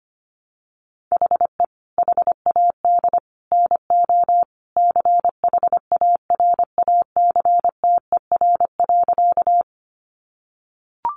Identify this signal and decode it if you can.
Morse code, heard as a keyed tone, about 25 wpm, 700 Hz. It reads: HE HAD NO CHARACTER.
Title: Sentences from Top 1200 Words 25wpm